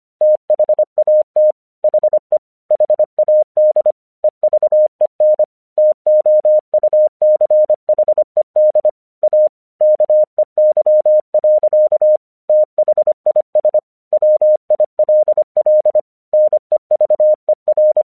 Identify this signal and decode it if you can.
Morse code, heard as a keyed tone, about 25 wpm, 610 Hz. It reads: THAT HE HAD EVEN TOUCHED A KEY. THIS WILL NEVER
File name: 1 English 25wpm / AQC30